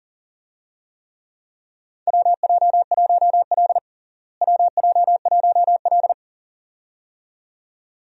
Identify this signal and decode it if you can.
Morse code, heard as a keyed tone, about 40 wpm, 700 Hz. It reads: WJ1L WJ1L
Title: USA Call Signs - 4 Characters - Repeated 2x - 40wpm